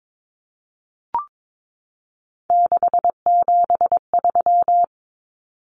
Morse code, heard as a keyed tone, about 22 wpm, 700 Hz. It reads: 673
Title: Common Cut Numbers - 3-Digit 22wpm